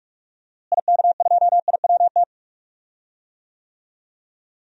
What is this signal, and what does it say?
Morse code, heard as a keyed tone, about 46 wpm, 700 Hz. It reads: IK2IWT
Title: DXCC-100 Call Signs - 6 Characters - Repeated 3x - 45wpm